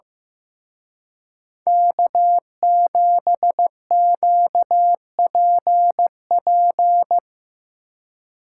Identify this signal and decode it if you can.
Morse code, heard as a keyed tone, about 16 wpm, 700 Hz. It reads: K7QPP
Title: Speed Racing - USA 1x3 Call Signs 15wpm